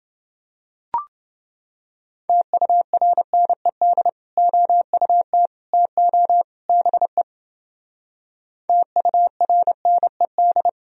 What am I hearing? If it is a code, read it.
Morse code, heard as a keyed tone, about 30 wpm, 700 Hz. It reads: TURNED OUT TO BE TURNED